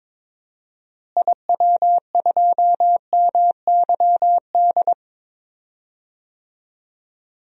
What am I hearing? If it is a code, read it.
Morse code, heard as a keyed tone, about 22 wpm, 700 Hz. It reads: IW2MYD
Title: DXCC-100 Call Signs - 6 Characters - Repeated 2x - 22wpm